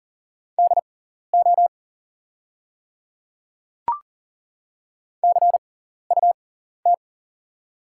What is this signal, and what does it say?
Morse code, heard as a keyed tone, about 40 wpm, spaced as for 14 wpm, 690 Hz. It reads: DO CUT
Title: 6x Character Spacing - Top 300 Words - 40wpm